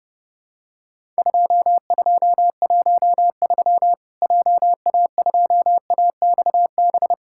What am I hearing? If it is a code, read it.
Morse code, heard as a keyed tone, about 30 wpm, 700 Hz. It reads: 2213 JA2AXB
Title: Worked All Europe DX Contest - QTC Exchange - 30wpm